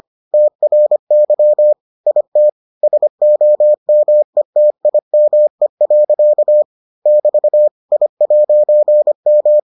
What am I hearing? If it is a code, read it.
Morse code, heard as a keyed tone, about 25 wpm, 600 Hz. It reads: TRY IT SOMETIME. <BT> I'M